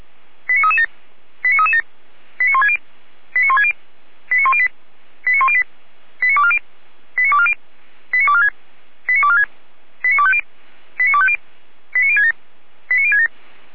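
Five-tone selective call (ZVEI), heard as quick runs of five tones, tones of about 70 ms. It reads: 89337 89337 89160 89160 89108 89108 89240 89240 89246 89246 89250 89250 89067 89067
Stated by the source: ZVEI five-tone selective call 89075, received by SDR.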